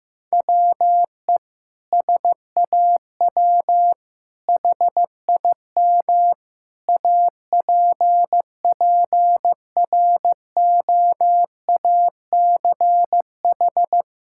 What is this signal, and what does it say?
Morse code, heard as a keyed tone, about 15 wpm, 700 Hz. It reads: WE SAW HIM APPROACH
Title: Sentences from Top 1000 Words - Review Part 6 - 15wpm